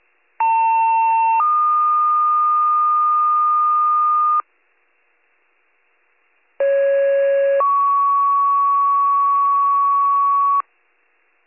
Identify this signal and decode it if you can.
Two-tone page, heard as a one-second tone, then a three-second tone, about 900 then 1220 Hz, about 550 then 1120 Hz. It is heard from 0.4 to 10.6 s.